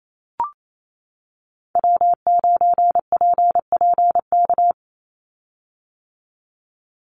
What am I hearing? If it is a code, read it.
Morse code, heard as a keyed tone, about 28 wpm, 700 Hz. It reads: W9PPK
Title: USA Call Signs 28wpm